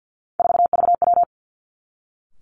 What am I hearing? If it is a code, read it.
Morse code, heard as a keyed tone, about 50 wpm, 700 Hz. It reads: 4VW
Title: Three Challenging Letter-Number Combinations 50wpm